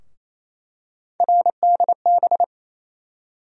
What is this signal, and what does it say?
Morse code, heard as a keyed tone, about 28 wpm, 700 Hz. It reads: RDB
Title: USA Call Sign Postfixes - 3 Letters 28wpm